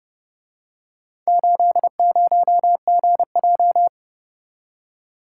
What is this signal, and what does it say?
Morse code, heard as a keyed tone, about 29 wpm, 700 Hz. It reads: 80GJ